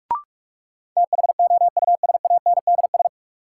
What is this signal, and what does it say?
Morse code, heard as a keyed tone, about 46 wpm, 700 Hz. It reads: THOUSANDS
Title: Sets of 4 Words - New Word Drill - 45wpm